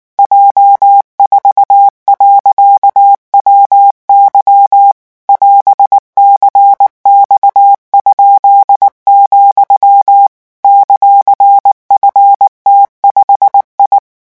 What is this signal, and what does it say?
Morse code, heard as a keyed tone, about 19 wpm, 800 Hz. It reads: J4.WY <AS>CX?, ;FT5I
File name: morse mp3 / revision15-vitesse 18 mots